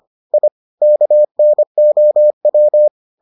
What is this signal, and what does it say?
Morse code, heard as a keyed tone, about 25 wpm, 600 Hz. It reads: I KNOW